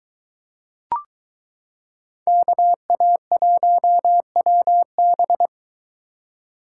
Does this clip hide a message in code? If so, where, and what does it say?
Yes: Morse code from 2.3 to 5.5 s, heard as a keyed tone, about 22 wpm, 700 Hz. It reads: KA1WB